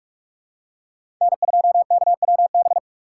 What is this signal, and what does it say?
Morse code, heard as a keyed tone, about 42 wpm, 700 Hz. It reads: N2KWB
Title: USA Call Signs - 5 Characters - 3x Character Spacing - 45wpm